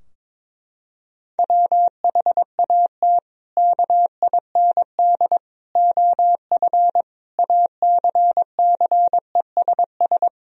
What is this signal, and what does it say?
Morse code, heard as a keyed tone, about 22 wpm, 700 Hz. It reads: WHAT KIND OF ACCESS
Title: Sentences from Top 1000 Words - Review Part 4 - 22wpm